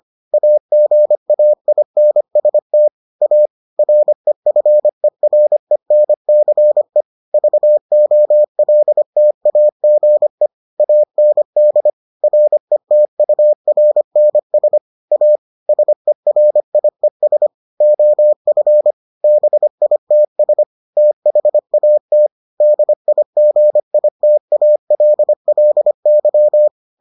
Morse code, heard as a keyed tone, about 25 wpm, 600 Hz. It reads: AGAINST A REFERENCE VOLTAGE AND RETURNS A SERIES OF BITS THAT DIGITALLY